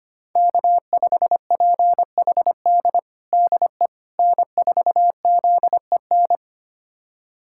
Morse code, heard as a keyed tone, about 25 wpm, 700 Hz. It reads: K5PHD DE N4ZEN
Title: USA Callsign DE Callsign - 5 Characters - 25wpm